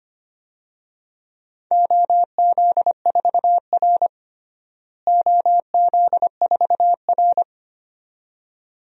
Morse code, heard as a keyed tone, about 25 wpm, 700 Hz. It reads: OZ4R OZ4R
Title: DXCC-100 Call Signs - 4 Characters - Repeated 2x - 25wpm